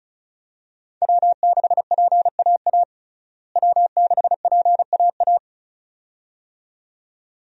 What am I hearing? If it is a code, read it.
Morse code, heard as a keyed tone, about 36 wpm, 700 Hz. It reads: W6PAA W6PAA
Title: USA Call Signs - 5 Characters - Repeated 2x - 35wpm